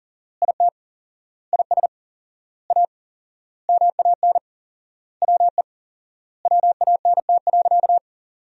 Morse code, heard as a keyed tone, about 40 wpm, 700 Hz. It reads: IT IS A MAN WE WANT.
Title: 4x Word-Spacing - Sentences from Top 200 Words - 40wpm